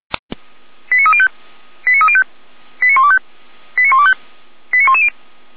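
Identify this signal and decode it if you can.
Five-tone selective call (ZVEI), heard as quick runs of five tones, tones of about 70 ms. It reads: 89305 89305 89125 89125 89110
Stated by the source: ZVEI five-tone call received by SDR, archived by the Orlamünde fire brigade (feuerwehr-orlamuende.org) server.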